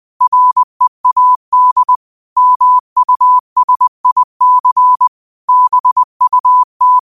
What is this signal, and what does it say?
Morse code, heard as a keyed tone, about 20 wpm, 1000 Hz. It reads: READ MUSIC BUT